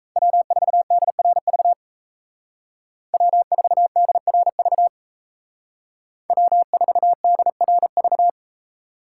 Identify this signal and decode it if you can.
Morse code, heard as a keyed tone, about 36 wpm, 700 Hz. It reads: W4DRV W4DRV W4DRV